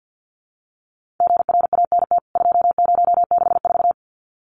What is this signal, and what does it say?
Morse code, heard as a keyed tone, about 49 wpm, 700 Hz. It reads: GRANT 2064
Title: CWT Exchange 50wpm